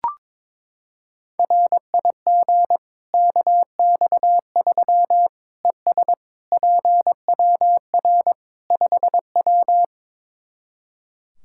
Morse code, heard as a keyed tone, about 22 wpm, 700 Hz. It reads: RIG KX3 ES PWR 5W
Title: Practice QSO - Exchange 2 - Info - 22wpm